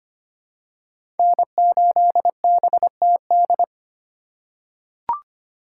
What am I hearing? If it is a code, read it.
Morse code, heard as a keyed tone, about 25 wpm, 700 Hz. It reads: N8BTD
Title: USA Call Signs - 3x Character Spacing - 25wpm